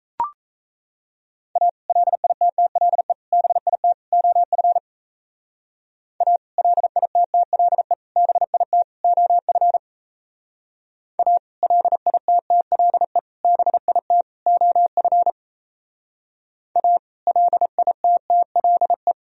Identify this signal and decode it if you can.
Morse code, heard as a keyed tone, about 34 wpm, 700 Hz. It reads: A LITTLE BIT OF A LITTLE BIT OF A LITTLE BIT OF A LITTLE
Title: Speed-Racing - Sets of 4 Words 28wpm